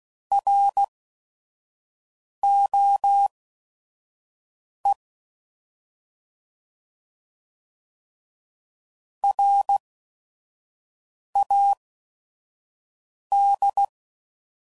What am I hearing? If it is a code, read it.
Morse code, heard as a keyed tone, about 16 wpm, spaced as for 5 wpm, 800 Hz. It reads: ROE RAD
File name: Prefixes / By First Letter / Rs